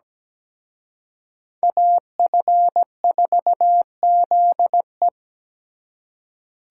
Morse code, heard as a keyed tone, about 20 wpm, 700 Hz. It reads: AF4ZE
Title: Speed Racing - USA 2x2 Call Signs 17wpm